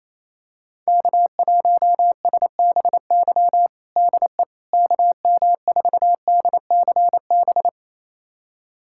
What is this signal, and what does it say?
Morse code, heard as a keyed tone, about 28 wpm, 700 Hz. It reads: K1SBY DE KM4DCB
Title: USA Callsign DE Callsign 28wpm